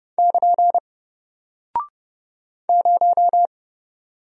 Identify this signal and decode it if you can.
Morse code, heard as a keyed tone, about 30 wpm, 700 Hz. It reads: <KN> 0
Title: Single Character (Letter:Number:Punctuation:Prosign) Alt 30wpm